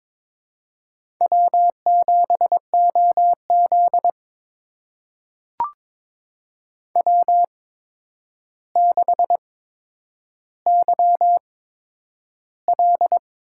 Morse code, heard as a keyed tone, about 22 wpm, spaced as for 9 wpm, 700 Hz. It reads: W7OZ W6YL
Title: USA Call Signs - 4 Characters - 8x Character Spacing - 22wpm